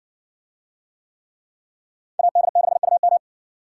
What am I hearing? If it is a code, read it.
Morse code, heard as a keyed tone, about 59 wpm, 700 Hz. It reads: AD6UM